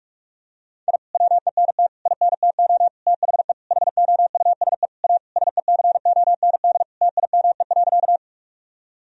Morse code, heard as a keyed tone, about 41 wpm, 700 Hz. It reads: I WENT INTO THE HOUSE A SECOND TIME.
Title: Sentences from Top 1000 Words - Review Part 1 - 45wpm